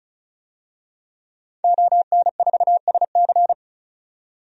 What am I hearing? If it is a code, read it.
Morse code, heard as a keyed tone, about 36 wpm, 700 Hz. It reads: ON4SC
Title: DXCC-100 Call Signs - 5 Characters - 35wpm